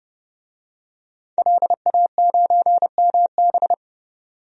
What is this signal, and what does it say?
Morse code, heard as a keyed tone, about 31 wpm, 690 Hz. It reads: LA9MB
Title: DXCC-100 Call Signs - 5 Characters - 30wpm